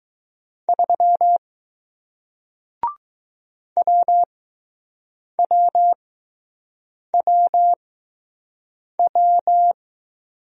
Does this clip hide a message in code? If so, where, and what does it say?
Yes: Morse code from 0.7 to 9.7 s, heard as a keyed tone, about 20 wpm, spaced as for 6 wpm, 700 Hz. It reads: 3 WWWW